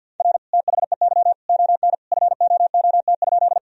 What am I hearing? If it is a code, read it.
Morse code, heard as a keyed tone, about 52 wpm, 700 Hz. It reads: R THEY ON FOOT?
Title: Sentences from Top 500 Words and Abbreviations 50wpm